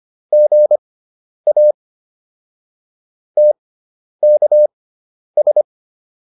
Morse code, heard as a keyed tone, about 25 wpm, spaced as for 10 wpm, 600 Hz. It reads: GA TKS
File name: QSO409 10